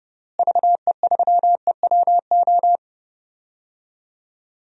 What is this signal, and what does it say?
Morse code, heard as a keyed tone, about 30 wpm, 700 Hz. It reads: VE3EWO
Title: DXCC-100 Call Signs - 6 Characters - Repeated 3x - 30wpm